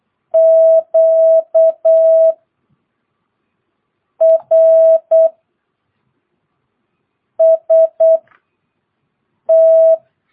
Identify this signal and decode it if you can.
Morse code, heard as a keyed tone, about 8 wpm, spaced as for 4 wpm, 650 Hz. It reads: QRST
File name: jh8cbh / 2022cw